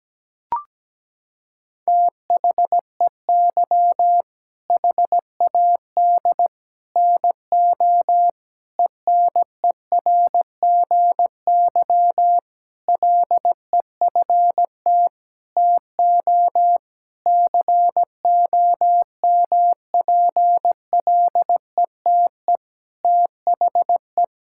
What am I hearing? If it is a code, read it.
Morse code, heard as a keyed tone, about 17 wpm, 700 Hz. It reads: THEY HAD NO ENERGY LEFT TO COMPLETE THE
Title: Sentences from Top 1000 Words - Review Part 6 - 17wpm